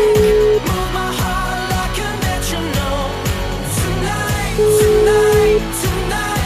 Ringback tone (Europe, 425 Hz), with noise.